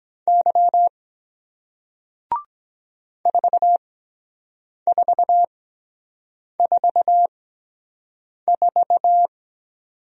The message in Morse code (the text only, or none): Y 4444